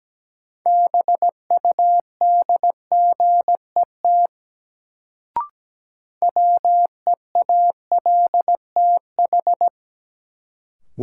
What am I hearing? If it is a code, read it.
Morse code, heard as a keyed tone, about 17 wpm, 700 Hz. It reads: BUDGET WEALTH
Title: Sentences from Top 1200 Words - New Word Drill Alt - 17wpm